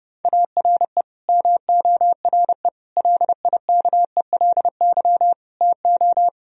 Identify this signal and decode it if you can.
Morse code, heard as a keyed tone, about 37 wpm, 700 Hz. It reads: ARE MORE LIKELY TO
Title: Speed-Racing - Sets of 5 Words 30wpm